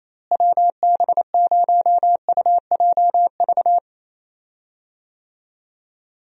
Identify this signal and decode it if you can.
Morse code, heard as a keyed tone, about 28 wpm, 700 Hz. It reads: WB0UJV
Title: USA Call Signs - 6 Characters - Repeated 2x - 28wpm